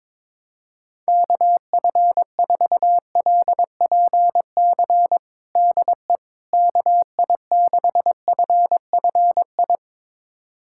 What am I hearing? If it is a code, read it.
Morse code, heard as a keyed tone, about 22 wpm, 700 Hz. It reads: KF4LPC DE KI6FFI
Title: USA Callsign DE Callsign - 6 Characters - 22wpm